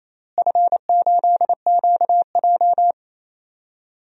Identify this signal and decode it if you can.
Morse code, heard as a keyed tone, about 27 wpm, 700 Hz. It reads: F8QJ
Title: DXCC-100 Call Signs - 4 Characters - Repeated 3x - 28wpm